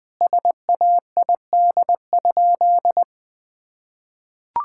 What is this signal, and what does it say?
Morse code, heard as a keyed tone, about 20 wpm, 700 Hz. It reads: SAID?
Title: Sentences from Top 1600 Words 20wpm